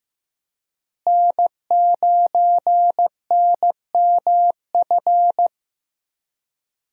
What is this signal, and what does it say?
Morse code, heard as a keyed tone, about 15 wpm, 700 Hz. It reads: N9NMF
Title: USA Call Signs - 5 Characters - Repeated 2x - 15wpm